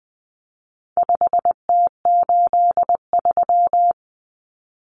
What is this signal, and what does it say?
Morse code, heard as a keyed tone, about 20 wpm, 700 Hz. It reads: 5T83
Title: Common Cut Numbers - 4-Digit 20wpm